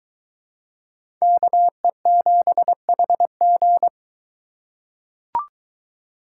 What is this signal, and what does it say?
Morse code, heard as a keyed tone, about 22 wpm, 700 Hz. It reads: KE7HG